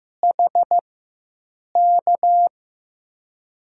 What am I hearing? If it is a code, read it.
Morse code, heard as a keyed tone, about 15 wpm, spaced as for 7 wpm, 700 Hz. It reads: HK
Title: USA Call Signs - 4x Character Spacing - 15wpm